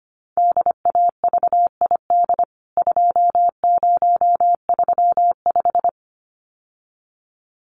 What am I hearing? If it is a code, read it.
Morse code, heard as a keyed tone, about 25 wpm, 700 Hz. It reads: DAVID 2035
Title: CWT Exchange 25wpm